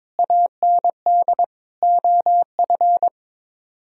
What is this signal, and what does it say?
Morse code, heard as a keyed tone, about 22 wpm, 700 Hz. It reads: AND OF
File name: Sets of 9 Words - 22wpm